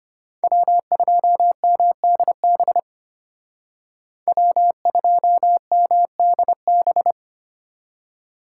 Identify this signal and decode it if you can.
Morse code, heard as a keyed tone, about 31 wpm, 700 Hz. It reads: W2MDB W2MDB
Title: Speed Racing - USA 1x3 Call Signs 25wpm